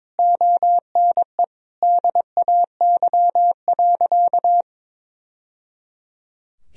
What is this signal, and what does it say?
Morse code, heard as a keyed tone, about 22 wpm, 700 Hz. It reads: ONE DAY.
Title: Sentences from Top 1000 Words - Review Part 3 - 22wpm